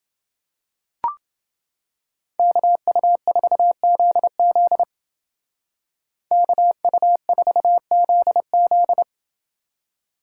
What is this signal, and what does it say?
Morse code, heard as a keyed tone, about 28 wpm, 700 Hz. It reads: KU4ZZ KU4ZZ